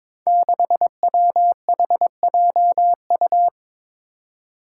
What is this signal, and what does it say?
Morse code, heard as a keyed tone, about 23 wpm, 700 Hz. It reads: 6WHJU
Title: Five Challenging Letter-Number Combinations 22wpm